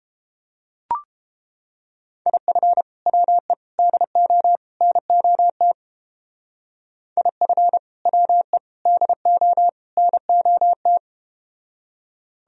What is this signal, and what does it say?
Morse code, heard as a keyed tone, about 30 wpm, 700 Hz. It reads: IF WE DO NOT IF WE DO NOT